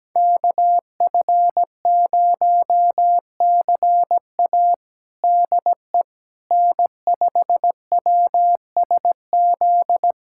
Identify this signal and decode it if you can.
Morse code, heard as a keyed tone, about 17 wpm, 700 Hz. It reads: KF0CA DE N5WSZ
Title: USA Callsign DE Callsign - 5 Characters - 17wpm